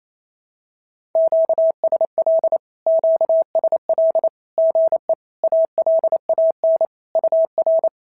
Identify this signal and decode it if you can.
Morse code, heard as a keyed tone, about 28 wpm, 650 Hz. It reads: QSL QSL GE ALAN UR
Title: SOTA QSO Practice 28wpm